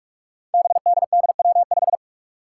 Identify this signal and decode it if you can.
Morse code, heard as a keyed tone, about 46 wpm, 700 Hz. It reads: BDDW5
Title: Five Challenging Letter-Number Combinations 45wpm